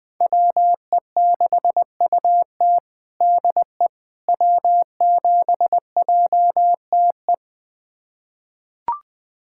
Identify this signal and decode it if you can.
Morse code, heard as a keyed tone, about 21 wpm, 700 Hz. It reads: WE6UT DE W7JTE
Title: USA Callsign DE Callsign - 5 Characters - 20wpm